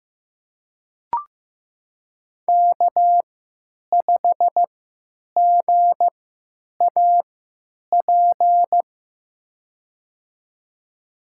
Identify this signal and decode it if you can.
Morse code, heard as a keyed tone, about 15 wpm, spaced as for 9 wpm, 700 Hz. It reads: K5GAP
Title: USA Call Signs - 5 Characters - 3x Character Spacing - 15wpm